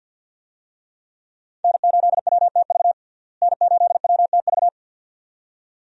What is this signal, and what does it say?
Morse code, heard as a keyed tone, about 52 wpm, 700 Hz. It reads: N8WTV N8WTV
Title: USA Call Signs - 5 Characters - Repeated 2x - 50wpm